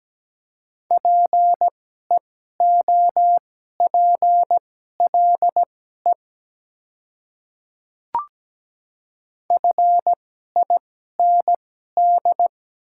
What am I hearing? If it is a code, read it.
Morse code, heard as a keyed tone, about 17 wpm, spaced as for 12 wpm, 700 Hz. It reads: PEOPLE FIND
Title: 2x Character Spacing - Top 100 Words - 17wpm